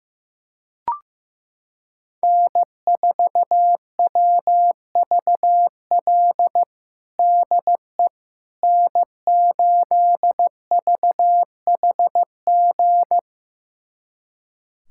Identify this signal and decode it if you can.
Morse code, heard as a keyed tone, about 15 wpm, 700 Hz. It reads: N4WVL DE N8VHG
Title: USA Callsign DE Callsign - 5 Characters - 15wpm